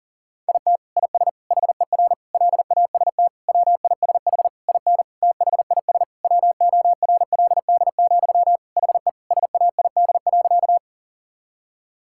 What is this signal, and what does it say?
Morse code, heard as a keyed tone, about 40 wpm, 700 Hz. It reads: IT IS HER LAST WISH IN THIS WORLD, HE SAID.
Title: Sentences from Top 1600 Words 40wpm